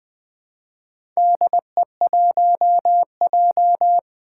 Morse code, heard as a keyed tone, about 20 wpm, 700 Hz. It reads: DE1J
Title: Four Challenging Letter-Number Combinations 20wpm